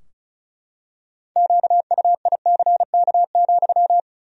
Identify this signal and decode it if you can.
Morse code, heard as a keyed tone, about 34 wpm, 700 Hz. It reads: QUICK,
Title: 2x Word-Spacing - Sentences from Top 500 Words - 35wpm